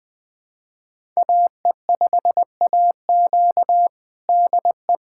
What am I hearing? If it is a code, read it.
Morse code, heard as a keyed tone, about 21 wpm, 700 Hz. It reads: AE5AQ DE
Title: USA Callsign DE Callsign - 5 Characters - 20wpm